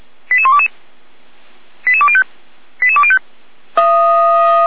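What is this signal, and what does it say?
Five-tone selective call (ZVEI), heard as quick runs of five tones, tones of about 70 ms. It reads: 88122 88205 88205
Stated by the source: ZVEI five-tone selective call received on an SDR, recorded by the Feuerwehr Orlamuende fire brigade.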